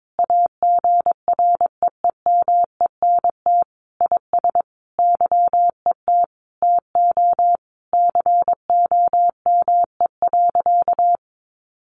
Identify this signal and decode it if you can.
Morse code, heard as a keyed tone, about 22 wpm, 700 Hz. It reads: AGREEMENT IS YET TO COME.
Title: Sentences from Top 1500 Words 22wpm